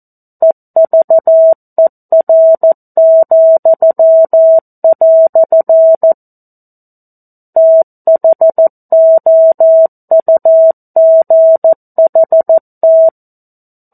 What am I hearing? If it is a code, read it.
Morse code, heard as a keyed tone, about 14 wpm, 650 Hz. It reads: EVER," THOUGHT
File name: 02-Chapter-02 14 WPM